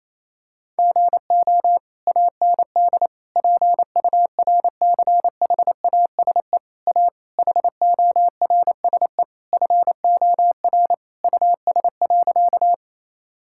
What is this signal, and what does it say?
Morse code, heard as a keyed tone, about 28 wpm, 700 Hz. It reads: GO AND PURCHASE A HORSE FOR US.
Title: Sentences from Top 1000 Words - Review Part 5 - 28wpm